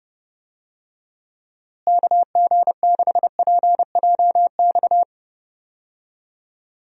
Morse code, heard as a keyed tone, about 31 wpm, 700 Hz. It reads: KG6PJX
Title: USA Call Signs - 6 Characters - Repeated 2x - 30wpm